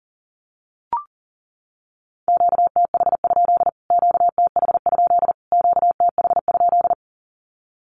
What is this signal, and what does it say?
Morse code, heard as a keyed tone, about 40 wpm, 700 Hz. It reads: QTH? QTH? QTH?
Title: Q-Codes Repeated 3x 40wpm